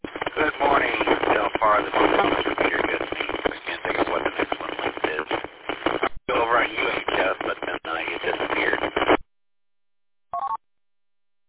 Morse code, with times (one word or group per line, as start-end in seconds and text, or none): none